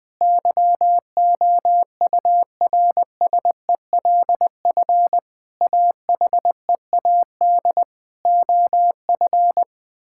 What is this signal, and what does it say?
Morse code, heard as a keyed tone, about 21 wpm, 700 Hz. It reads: YOURSELF AHEAD OF
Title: Sentences from Top 1000 Words - Review Part 6 - 20wpm